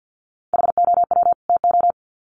Morse code, heard as a keyed tone, about 50 wpm, 700 Hz. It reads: HOW TO
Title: Sets of 2 Words 50wpm